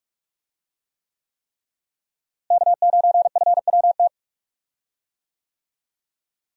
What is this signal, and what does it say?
Morse code, heard as a keyed tone, about 46 wpm, 700 Hz. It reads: K9FWT
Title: USA Call Signs - 5 Characters - Repeated 3x - 45wpm